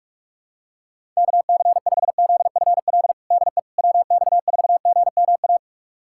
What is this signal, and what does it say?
Morse code, heard as a keyed tone, about 46 wpm, 700 Hz. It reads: KC5ZFL DE WX4GMA
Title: USA Callsign DE Callsign - 6 Characters - 45wpm